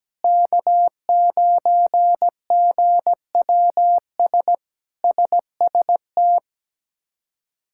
Morse code, heard as a keyed tone, about 17 wpm, 700 Hz. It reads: K9GWS SST
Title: K1USN SST QSO Practice - 17wpm